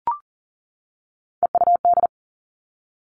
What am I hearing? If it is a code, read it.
Morse code, heard as a keyed tone, about 40 wpm, 700 Hz. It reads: EUD
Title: Three Challenging Letter-Number Combinations 40wpm